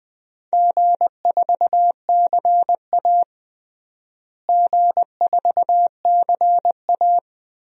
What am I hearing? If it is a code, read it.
Morse code, heard as a keyed tone, about 20 wpm, 700 Hz. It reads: G4CA G4CA
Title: DXCC-100 Call Signs - 4 Characters - Repeated 3x - 20wpm